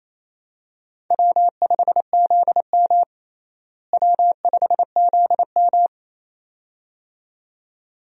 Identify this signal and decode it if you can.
Morse code, heard as a keyed tone, about 28 wpm, 700 Hz. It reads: W5ZM W5ZM
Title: USA Call Signs - 4 Characters - Repeated 2x - 28wpm